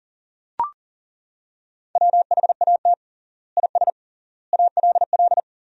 Morse code, heard as a keyed tone, about 40 wpm, 700 Hz. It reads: WHAT IS ALL
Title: 3x Word-Spacing - Sentences from Top 100 Words - 40wpm